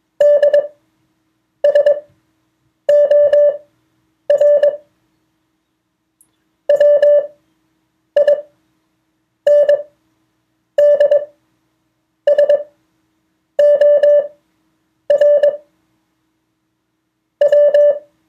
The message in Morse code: DSOR WINDSOR W